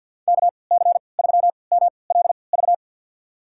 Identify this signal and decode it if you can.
Morse code, heard as a keyed tone, about 50 wpm, spaced as for 28 wpm, 700 Hz. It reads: KC3MLV
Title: USA Call Signs - 6 Characters - 3x Character Spacing - 50wpm